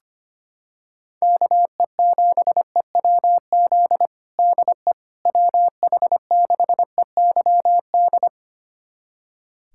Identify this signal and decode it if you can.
Morse code, heard as a keyed tone, about 25 wpm, 700 Hz. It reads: KE7EWZ DE WH6EYD
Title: USA Callsign DE Callsign 25wpm